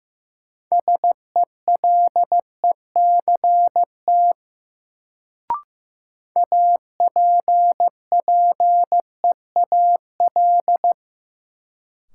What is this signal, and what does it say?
Morse code, heard as a keyed tone, about 16 wpm, 700 Hz. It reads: SELECT APPEAL
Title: Sentences from Top 1300 Words - New Word Drill - 15wpm